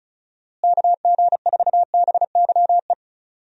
Morse code, heard as a keyed tone, about 36 wpm, 700 Hz. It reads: KG4BYE
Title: USA Call Signs - 6 Characters - Repeated 3x - 35wpm